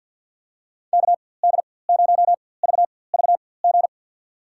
Morse code, heard as a keyed tone, about 53 wpm, spaced as for 24 wpm, 690 Hz. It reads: KD0VVG
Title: USA Call Signs - 4x Character Spacing - 50wpm